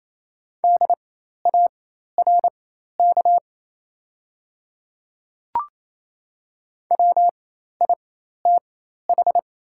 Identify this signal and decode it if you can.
Morse code, heard as a keyed tone, about 28 wpm, spaced as for 13 wpm, 700 Hz. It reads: DARK WITH
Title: 4x Character Spacing - Top 500 Words - 28wpm